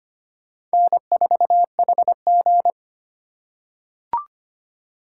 Morse code, heard as a keyed tone, about 25 wpm, 700 Hz. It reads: N4HG
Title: USA Call Signs - 2x Character Spacing - 25wpm